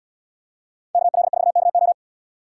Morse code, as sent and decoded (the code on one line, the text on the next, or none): -.- .-.. ....- -.- ---
KL4KO